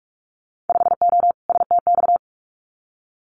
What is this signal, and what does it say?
Morse code, heard as a keyed tone, about 44 wpm, 700 Hz. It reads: 5O STX
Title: Winter Field Day Exchanges 45wpm